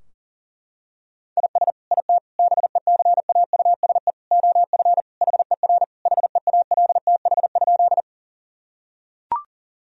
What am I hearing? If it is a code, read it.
Morse code, heard as a keyed tone, about 40 wpm, 700 Hz. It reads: IS IT BECAUSE OF HER HEALTH?
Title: Sentences from Top 1000 Words - Review Part 3 - 40wpm